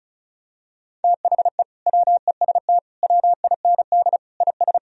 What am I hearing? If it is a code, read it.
Morse code, heard as a keyed tone, about 36 wpm, 700 Hz. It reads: THE WEST WIND IS
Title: Sentences from Top 1000 Words - Review Part 2 - 35wpm